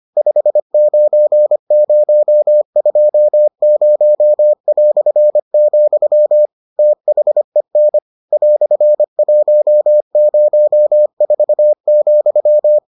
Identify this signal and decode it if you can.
Morse code, heard as a keyed tone, about 25 wpm, 600 Hz. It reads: 59020", THEN "104,